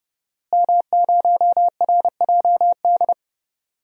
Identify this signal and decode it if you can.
Morse code, heard as a keyed tone, about 30 wpm, 700 Hz. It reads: M0RJD
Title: DXCC-100 Call Signs - 5 Characters - Repeated 3x - 30wpm